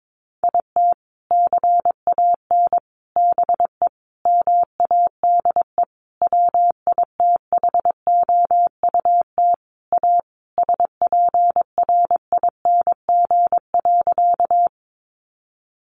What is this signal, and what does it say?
Morse code, heard as a keyed tone, about 22 wpm, 700 Hz. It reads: IT CAN BE MADE WITHOUT A SPRING.
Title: Sentences from Top 1200 Words 22wpm